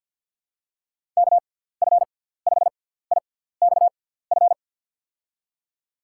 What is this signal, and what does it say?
Morse code, heard as a keyed tone, about 52 wpm, spaced as for 17 wpm, 700 Hz. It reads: KF5IXF
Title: USA Call Signs - 6 Characters - 6x Character Spacing - 50wpm